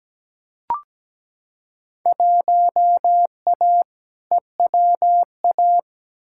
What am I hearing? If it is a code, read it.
Morse code, heard as a keyed tone, about 17 wpm, 700 Hz. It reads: 1A EWA
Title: Field Day Exchanges Encore - 17wpm